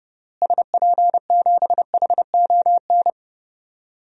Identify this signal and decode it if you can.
Morse code, heard as a keyed tone, about 30 wpm, 700 Hz. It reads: SP7HON
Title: DXCC-100 Call Signs - 6 Characters - Repeated 2x - 30wpm